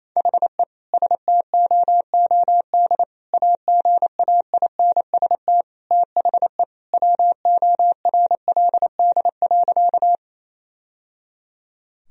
Morse code, heard as a keyed tone, about 28 wpm, 700 Hz. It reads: HE STOOD AGAINST THE WORLD.
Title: Sentences from Top 1000 Words - Review Part 2 - 28wpm